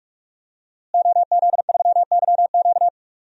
Morse code, heard as a keyed tone, about 44 wpm, 700 Hz. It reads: OZ3YQ
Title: DXCC-100 Call Signs - 5 Characters - Repeated 2x - 45wpm